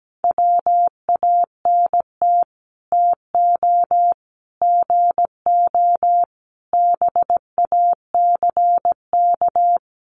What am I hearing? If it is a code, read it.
Morse code, heard as a keyed tone, about 17 wpm, 700 Hz. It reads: WANT TO GO BACK
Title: Sets of 5 Words 17wpm